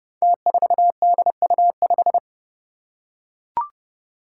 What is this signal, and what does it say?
Morse code, heard as a keyed tone, about 30 wpm, 700 Hz. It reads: T4DU5
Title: Five Challenging Letter-Number Combinations 30wpm